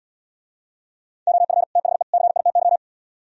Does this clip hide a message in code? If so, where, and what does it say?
Yes: Morse code from 1.3 to 2.8 s, heard as a keyed tone, about 76 wpm, 700 Hz. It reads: OF THE CITY